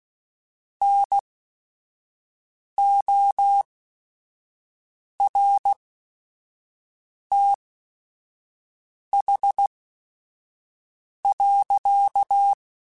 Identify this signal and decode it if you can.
Morse code, heard as a keyed tone, about 16 wpm, spaced as for 5 wpm, 800 Hz. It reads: NORTH.
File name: Weather Sets / Weather Sets-mp3s / Weather 4 24 @5